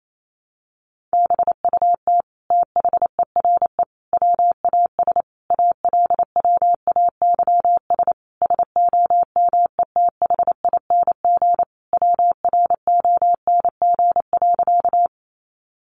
Morse code, heard as a keyed tone, about 28 wpm, 690 Hz. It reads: BUT THERE WAS ALWAYS SOMETHING WRONG.
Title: Sentences from Top 900 Words 28wpm